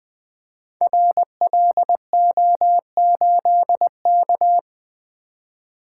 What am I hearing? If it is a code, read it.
Morse code, heard as a keyed tone, about 20 wpm, 700 Hz. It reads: RLO8K
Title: Code Groups - 5 Letter:Numbers - 20wpm